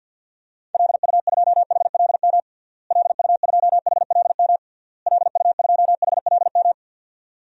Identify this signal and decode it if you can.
Morse code, heard as a keyed tone, about 52 wpm, 700 Hz. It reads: LU2HLM LU2HLM LU2HLM
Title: DXCC-100 Call Signs - 6 Characters - Repeated 3x - 50wpm